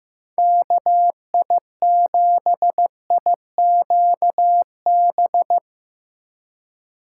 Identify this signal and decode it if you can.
Morse code, heard as a keyed tone, about 16 wpm, 700 Hz. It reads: KI7IQB
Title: Speed Racing - USA 2x3 Call Signs 15wpm